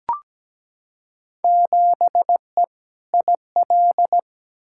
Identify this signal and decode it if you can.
Morse code, heard as a keyed tone, about 17 wpm, 700 Hz. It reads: 7E IL
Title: Field Day Exchanges Encore - 17wpm